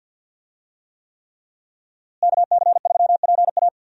Morse code, heard as a keyed tone, about 52 wpm, 700 Hz. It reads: KC3PA
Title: USA Call Signs - 5 Characters - Repeated 3x - 50wpm